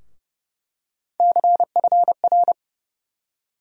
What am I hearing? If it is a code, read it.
Morse code, heard as a keyed tone, about 30 wpm, 700 Hz. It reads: CFR
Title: USA Call Sign Postfixes - 3 Letters 30wpm